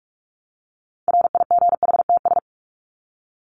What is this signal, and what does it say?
Morse code, heard as a keyed tone, about 44 wpm, 700 Hz. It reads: RIGHTS
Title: ES:S Suffix - Top 100 Words with Base Words 45wpm